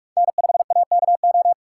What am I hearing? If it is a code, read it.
Morse code, heard as a keyed tone, about 42 wpm, 700 Hz. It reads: N5AKO